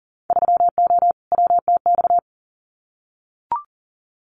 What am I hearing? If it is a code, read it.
Morse code, heard as a keyed tone, about 40 wpm, 700 Hz. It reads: 3O WTX
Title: Winter Field Day Exchanges Encore 40wpm